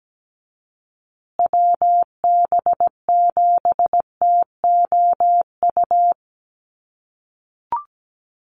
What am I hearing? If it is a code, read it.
Morse code, heard as a keyed tone, about 17 wpm, 700 Hz. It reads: WB7TOU
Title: USA Call Signs Repeated 2x 17wpm